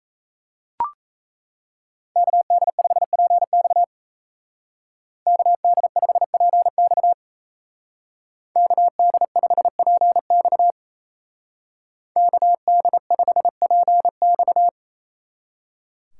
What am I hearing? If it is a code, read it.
Morse code, heard as a keyed tone, about 33 wpm, 700 Hz. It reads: KD5PX KD5PX KD5PX KD5PX
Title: Speed Racing - USA 2x2 Call Signs 28wpm